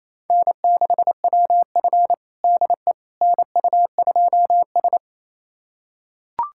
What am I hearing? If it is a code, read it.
Morse code, heard as a keyed tone, about 28 wpm, 700 Hz. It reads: N6WF DE NU2S
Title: USA Callsign DE Callsign - 4 Characters - 28wpm